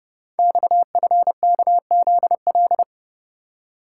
Morse code, heard as a keyed tone, about 30 wpm, 700 Hz. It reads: XFKZL
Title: Code Groups - 5 Letters - 30wpm